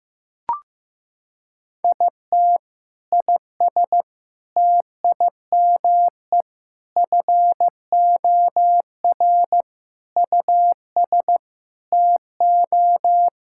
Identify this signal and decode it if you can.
Morse code, heard as a keyed tone, about 16 wpm, 700 Hz. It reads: IT IS TIME FOR US TO
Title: Sentences from Top 1000 Words - Review Part 6 - 15wpm